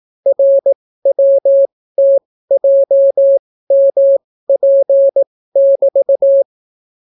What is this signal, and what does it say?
Morse code, heard as a keyed tone, about 18 wpm, spaced as for 14 wpm, 550 Hz. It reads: RWTJMP<BT>